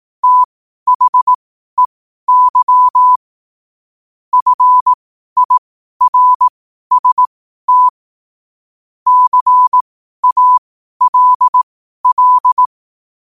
Morse code, heard as a keyed tone, about 18 wpm, spaced as for 12 wpm, 1000 Hz. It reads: THEY FIRST CALL
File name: qotdfiles / 12wpm